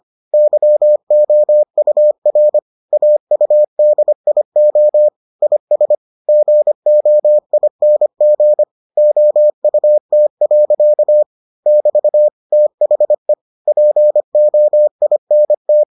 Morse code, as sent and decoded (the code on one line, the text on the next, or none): -.-- --- ..- .-. / .- ..- -.. .. --- / .. ... / --. --- .. -. --. / --- ..- - .-.-.- / -...- / - .... / .--. --- .. -. -
YOUR AUDIO IS GOING OUT. <BT> THE POINT